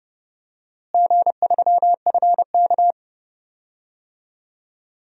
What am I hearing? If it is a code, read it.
Morse code, heard as a keyed tone, about 30 wpm, 700 Hz. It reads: G3FK
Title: DXCC-100 Call Signs - 4 Characters - Repeated 2x - 30wpm